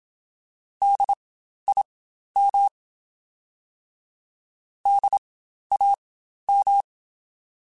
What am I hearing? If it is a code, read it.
Morse code, heard as a keyed tone, about 27 wpm, spaced as for 12 wpm, 800 Hz. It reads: DIM DAM
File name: Spaced Alphas / Spaced Alphas mp3s / Ds sp @25